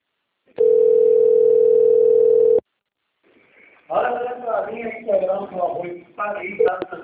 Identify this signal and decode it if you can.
Ringback tone (North America), 440 + 480 Hz, with speech.
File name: rg-1600-unknown- - - .33889